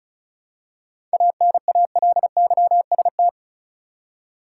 Morse code, heard as a keyed tone, about 36 wpm, 700 Hz. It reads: ANALYST
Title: Practice QSO - Exchange 3 - More Info - 35wpm